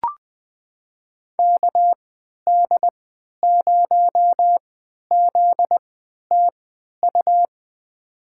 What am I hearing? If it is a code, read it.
Morse code, heard as a keyed tone, about 20 wpm, spaced as for 11 wpm, 700 Hz. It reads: KD0ZTU
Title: USA Call Signs - 6 Characters - 3x Character Spacing - 20wpm